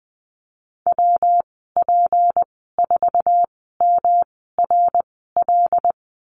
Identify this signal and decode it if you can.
Morse code, heard as a keyed tone, about 20 wpm, spaced as for 14 wpm, 700 Hz. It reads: WP4MRL
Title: 173 Lesson 28g - Number 9 - Call Signs 20wpm